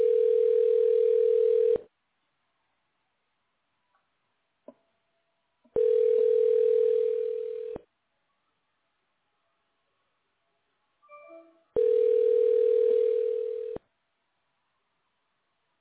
Ringback tone (North America), 440 + 480 Hz.